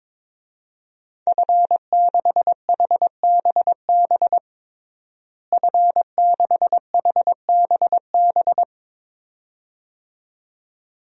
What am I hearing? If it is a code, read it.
Morse code, heard as a keyed tone, about 22 wpm, 700 Hz. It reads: F6HBB F6HBB
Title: DXCC-100 Call Signs - 5 Characters - Repeated 2x - 22wpm